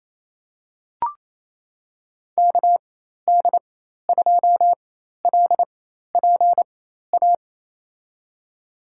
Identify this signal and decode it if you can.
Morse code, heard as a keyed tone, about 27 wpm, spaced as for 13 wpm, 700 Hz. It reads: KD2LPA
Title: USA Call Signs - 6 Characters - 4x Character Spacing - 28wpm